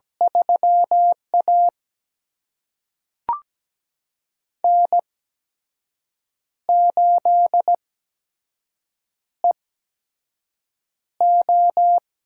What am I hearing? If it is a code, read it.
Morse code, heard as a keyed tone, about 17 wpm, spaced as for 6 wpm, 700 Hz. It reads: 3A N8EO
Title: USA Call Signs - 8x Character Spacing - 17wpm